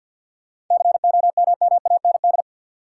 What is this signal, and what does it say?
Morse code, heard as a keyed tone, about 48 wpm, 700 Hz. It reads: COMMAND
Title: Sentences from Top 1600 Words - New Word Drill Alt - 50wpm